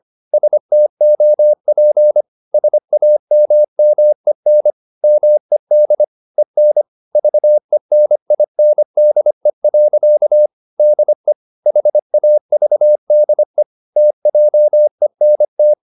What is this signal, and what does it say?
Morse code, heard as a keyed tone, about 25 wpm, 600 Hz. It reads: STOP SAMMEN MED EN VENINDE. DE HAVDE TJENT
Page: Ferie 25 wpm